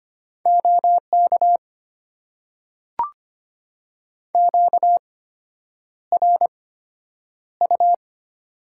Morse code, heard as a keyed tone, about 25 wpm, spaced as for 9 wpm, 700 Hz. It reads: OK QRU
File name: Top 90 QSO Elements - 8x Character Spacing - 25wpm